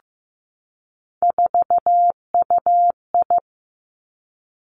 Morse code, heard as a keyed tone, about 15 wpm, 700 Hz. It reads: 4UI
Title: Three Challenging Letter-Number Combinations 15wpm